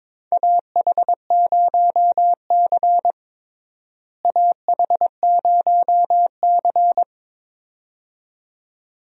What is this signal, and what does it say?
Morse code, heard as a keyed tone, about 21 wpm, 700 Hz. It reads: AH0C AH0C
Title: USA Call Signs - 4 Characters - Repeated 2x - 22wpm